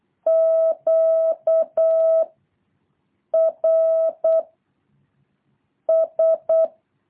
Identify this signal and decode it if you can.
Morse code, heard as a keyed tone, about 8 wpm, 660 Hz. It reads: Q R S
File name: jh8cbh / 2023cw